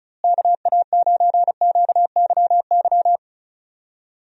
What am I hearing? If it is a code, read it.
Morse code, heard as a keyed tone, about 36 wpm, 700 Hz. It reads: KA9QYY
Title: USA Call Signs - 6 Characters - Repeated 2x - 35wpm